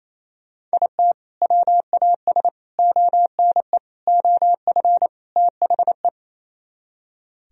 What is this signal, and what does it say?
Morse code, heard as a keyed tone, about 38 wpm, spaced as for 33 wpm, 700 Hz. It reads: IT WAS ONE OF THE
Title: Speed-Racing - Sets of 5 Words 28wpm